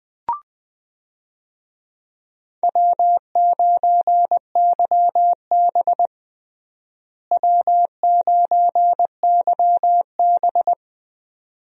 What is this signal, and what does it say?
Morse code, heard as a keyed tone, about 20 wpm, 700 Hz. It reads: W9YB W9YB
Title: USA Call Signs - 4 Characters - Repeated 2x - 20wpm